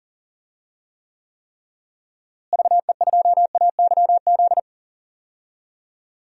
Morse code, heard as a keyed tone, about 40 wpm, 700 Hz. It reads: VE2AYZ
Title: DXCC-100 Call Signs - 6 Characters - Repeated 3x - 40wpm